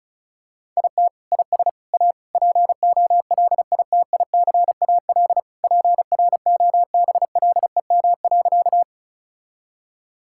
Morse code, heard as a keyed tone, about 36 wpm, 700 Hz. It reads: IT IS A POLITICAL PROBLEM.
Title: Sentences from Top 1000 Words - Review Part 5 - 35wpm